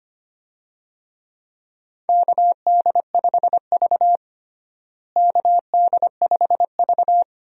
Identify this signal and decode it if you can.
Morse code, heard as a keyed tone, about 25 wpm, 700 Hz. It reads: KD5V KD5V
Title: USA Call Signs - 4 Characters - Repeated 3x - 25wpm